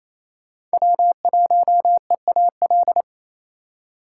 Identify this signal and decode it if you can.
Morse code, heard as a keyed tone, about 34 wpm, 700 Hz. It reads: W1EAL
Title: Speed Racing - USA Call Signs 28wpm